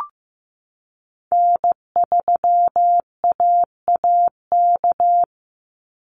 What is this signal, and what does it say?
Morse code, heard as a keyed tone, about 15 wpm, 700 Hz. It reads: N3AAK
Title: Call Signs Repeated 3x - 15wpm